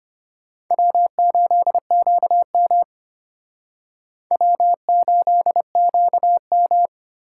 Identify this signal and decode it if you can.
Morse code, heard as a keyed tone, about 33 wpm, 700 Hz. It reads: W8QM W8QM